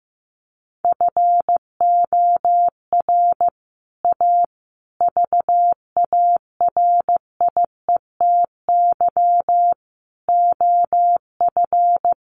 Morse code, heard as a keyed tone, about 15 wpm, 700 Hz. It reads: FOR A VARIETY OF
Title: Sets of 5 Words Encore 15wpm